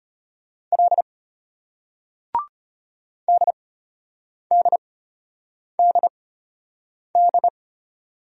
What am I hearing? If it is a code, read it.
Morse code, heard as a keyed tone, about 32 wpm, spaced as for 8 wpm, 700 Hz. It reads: L DDDD